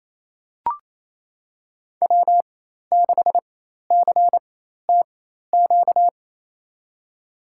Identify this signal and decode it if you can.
Morse code, heard as a keyed tone, about 28 wpm, spaced as for 13 wpm, 700 Hz. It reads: W6CTQ
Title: USA Call Signs - 5 Characters - 4x Character Spacing - 28wpm